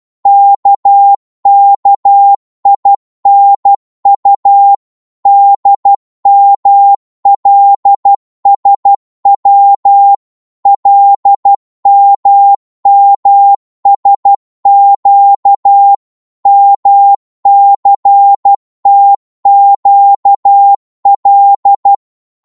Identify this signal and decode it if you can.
Morse code, heard as a keyed tone, about 12 wpm, 800 Hz. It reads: KKINU DMLSW LMMSQ MCTQL
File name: cwmp3 / ltr-12wpm